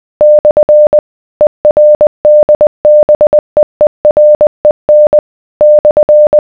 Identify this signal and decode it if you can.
Morse code, heard as a keyed tone, about 20 wpm, 600 Hz. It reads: / ERDBEEREN /